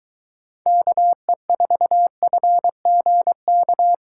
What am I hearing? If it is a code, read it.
Morse code, heard as a keyed tone, about 22 wpm, 700 Hz. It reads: KE4FGK